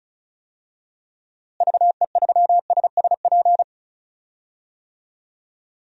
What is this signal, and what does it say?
Morse code, heard as a keyed tone, about 34 wpm, 700 Hz. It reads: VE3SSP
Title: DXCC-100 Call Signs - 6 Characters - Repeated 3x - 35wpm